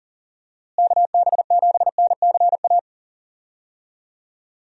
Morse code, heard as a keyed tone, about 37 wpm, 700 Hz. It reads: KB7NCA
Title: USA Call Signs - 6 Characters - Repeated 2x - 40wpm